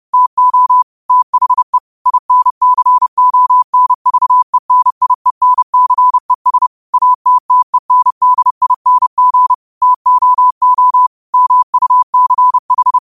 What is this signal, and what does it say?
Morse code, heard as a keyed tone, about 30 wpm, 1000 Hz. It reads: TO THE INCONVENIENCES ATTENDING TOO MUCH
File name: qotdfiles / 30wpm